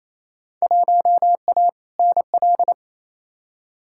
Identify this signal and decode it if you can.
Morse code, heard as a keyed tone, about 28 wpm, 700 Hz. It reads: 1A NL
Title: Field Day Exchanges Encore - 28wpm